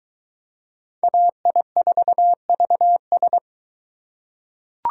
Morse code, heard as a keyed tone, about 23 wpm, 700 Hz. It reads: AI4VS